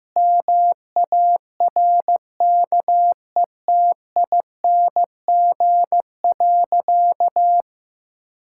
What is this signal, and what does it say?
Morse code, heard as a keyed tone, about 15 wpm, 700 Hz. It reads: MARKETING.
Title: Sentences from Top 1000 Words - Review Part 5 - 15wpm